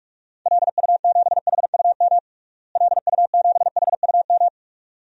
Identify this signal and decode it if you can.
Morse code, heard as a keyed tone, about 45 wpm, 700 Hz. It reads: LU7HUM LU7HUM
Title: DXCC-100 Call Signs - 6 Characters - Repeated 2x - 45wpm